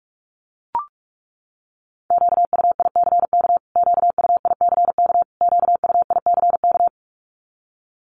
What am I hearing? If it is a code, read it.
Morse code, heard as a keyed tone, about 44 wpm, 700 Hz. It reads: QUICK QUICK QUICK
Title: Top 500 Words Repeated 45wpm